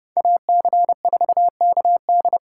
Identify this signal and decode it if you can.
Morse code, heard as a keyed tone, about 35 wpm, 700 Hz. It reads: AC4KD
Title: Speed Racing - USA 2x2 Call Signs 30wpm